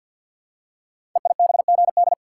Morse code, heard as a keyed tone, about 46 wpm, 700 Hz. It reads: EIBGD
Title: Code Groups - 5 Letters - 50wpm